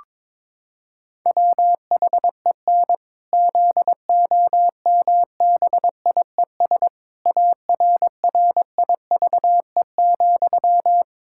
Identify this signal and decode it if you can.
Morse code, heard as a keyed tone, about 22 wpm, 700 Hz. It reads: WHEN ZOMBIES ARRIVE,
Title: Pangram Challenge v2 - 22wpm